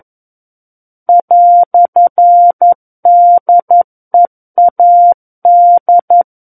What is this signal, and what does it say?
Morse code, heard as a keyed tone, about 11 wpm, 700 Hz. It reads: "DEAD
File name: 11 WPM